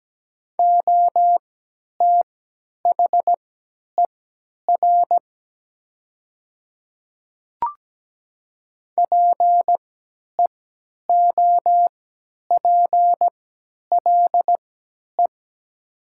Morse code, heard as a keyed tone, about 17 wpm, spaced as for 10 wpm, 700 Hz. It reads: OTHER PEOPLE
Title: 3x Character Spacing - Top 100 Words - 17wpm